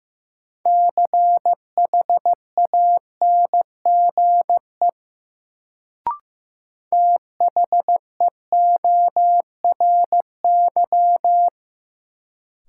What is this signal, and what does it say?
Morse code, heard as a keyed tone, about 15 wpm, 700 Hz. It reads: CHANGE THEORY
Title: Sentences from Top 600 Words - New Word Drill - 15wpm